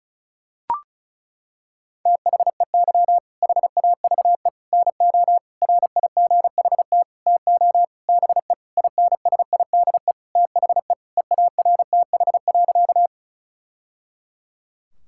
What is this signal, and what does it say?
Morse code, heard as a keyed tone, about 34 wpm, 700 Hz. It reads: THEY HAVE NO RIGHT TO BE INSIDE THE EARTH.
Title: Sentences from Top 1000 Words - Review Part 6 - 35wpm